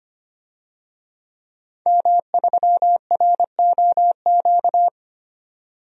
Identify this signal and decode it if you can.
Morse code, heard as a keyed tone, about 25 wpm, 700 Hz. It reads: M3ROQ
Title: DXCC-100 Call Signs - 5 Characters - 25wpm